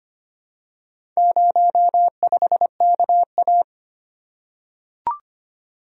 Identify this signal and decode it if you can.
Morse code, heard as a keyed tone, about 25 wpm, 700 Hz. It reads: 05KA
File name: practice / 25wpm / Four Letter-Number Combinations - 25wpm